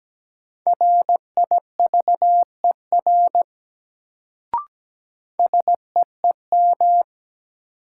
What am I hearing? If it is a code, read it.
Morse code, heard as a keyed tone, about 17 wpm, 700 Hz. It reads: RIVER SEEM
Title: Sentences from Top 300 Words - New Word Drill - 17wpm